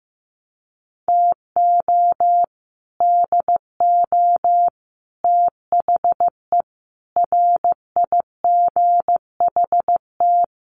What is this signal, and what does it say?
Morse code, heard as a keyed tone, about 15 wpm, 700 Hz. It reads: TO DO THE RIGHT
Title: Sets of 5 Words 15wpm